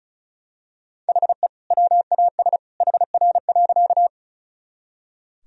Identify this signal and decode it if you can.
Morse code, heard as a keyed tone, about 34 wpm, 700 Hz. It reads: HE WAS HR.
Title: Sentences from Top 200 Words and Abbreviations 35wpm